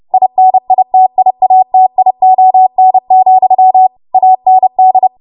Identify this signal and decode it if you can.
Morse code, heard as a keyed tone, about 30 wpm, 750 Hz. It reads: INITIATION, AND
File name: Morse / 30WPM